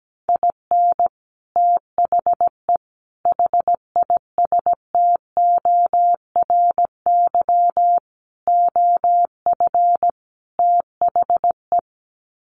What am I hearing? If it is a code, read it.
Morse code, heard as a keyed tone, about 17 wpm, 700 Hz. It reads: IN THE HISTORY OF THE
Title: Sets of 5 Words 17wpm